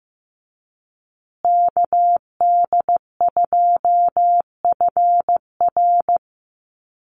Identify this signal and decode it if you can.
Morse code, heard as a keyed tone, about 15 wpm, 700 Hz. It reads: KD2FR
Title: USA Call Signs Repeated 2x 15wpm